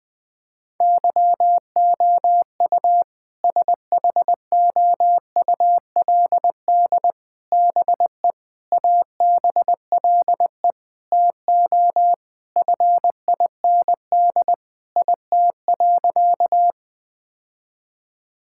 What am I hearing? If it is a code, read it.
Morse code, heard as a keyed tone, about 20 wpm, 700 Hz. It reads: YOU SHOULD BE ABLE TO FIND IT.
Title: Sentences with 4-Word Phrases - 20wpm